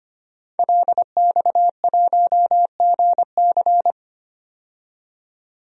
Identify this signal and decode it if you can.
Morse code, heard as a keyed tone, about 25 wpm, 700 Hz. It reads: LX1GC
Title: DXCC-100 Call Signs - 5 Characters - 25wpm